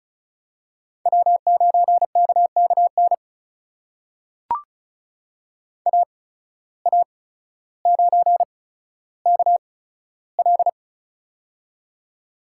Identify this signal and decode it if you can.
Morse code, heard as a keyed tone, about 36 wpm, spaced as for 15 wpm, 700 Hz. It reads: W9KKN AA9KL
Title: USA Call Signs - 5 Characters - 8x Character Spacing - 35wpm